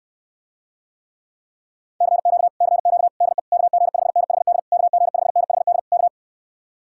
Morse code, heard as a keyed tone, about 69 wpm, 700 Hz. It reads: CQ CQ DE KG5NHD KG5NHD K